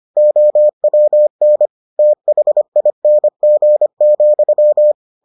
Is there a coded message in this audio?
Yes: Morse code from 0.2 to 4.9 s, heard as a keyed tone, about 25 wpm, 600 Hz. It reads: OWN THING,